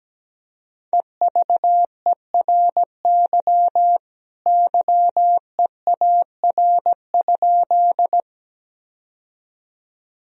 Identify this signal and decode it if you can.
Morse code, heard as a keyed tone, about 17 wpm, 700 Hz. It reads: EVERY YEAR?
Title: Sentences from Top 1000 Words - Review Part 1 - 17wpm